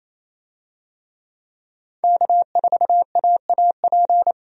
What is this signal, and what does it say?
Morse code, heard as a keyed tone, about 28 wpm, 700 Hz. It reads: K4AAP
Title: USA Call Signs - 5 Characters - 28wpm